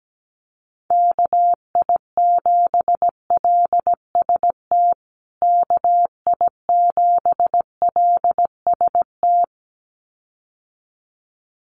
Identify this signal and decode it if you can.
Morse code, heard as a keyed tone, about 17 wpm, 700 Hz. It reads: KI7LST KI7LST
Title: USA Call Signs Repeated 2x 17wpm